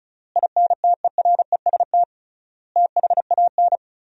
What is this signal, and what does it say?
Morse code, heard as a keyed tone, about 34 wpm, 700 Hz. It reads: INTEREST THAN
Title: 3x Word-Spacing - Sentences from Top 400 Words - 35wpm